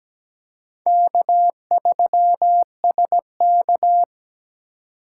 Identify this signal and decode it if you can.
Morse code, heard as a keyed tone, about 17 wpm, 700 Hz. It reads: K3SK
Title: USA Call Signs - 4 Characters - 3x Character Spacing - 17wpm